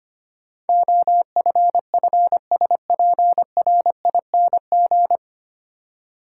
Morse code, heard as a keyed tone, about 25 wpm, 700 Hz. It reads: OFFSPRING
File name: practice / 25wpm / 9 Letter Words - First 3 Letters Given - 25wpm